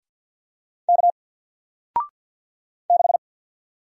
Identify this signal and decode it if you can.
Morse code, heard as a keyed tone, about 50 wpm, 700 Hz. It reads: K 6
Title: Single Character (Letter:Number:Punctuation:Prosign) 50wpm